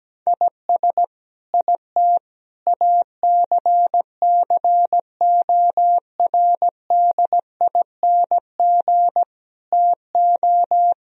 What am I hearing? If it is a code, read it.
Morse code, heard as a keyed tone, about 17 wpm, 700 Hz. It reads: IS IT ACCORDING TO
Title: Sentences from Top 1000 Words - Review Part 4 - 17wpm